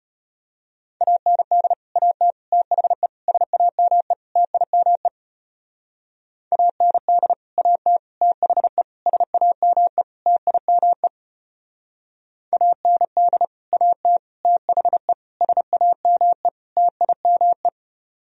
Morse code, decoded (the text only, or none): AND AT THE SAME TIME AND AT THE SAME TIME AND AT THE SAME TIME